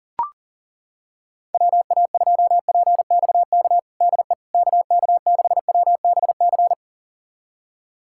Morse code, heard as a keyed tone, about 40 wpm, 700 Hz. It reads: WA2PXK DE KK6WBC
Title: USA Callsign DE Callsign - 6 Characters - 40wpm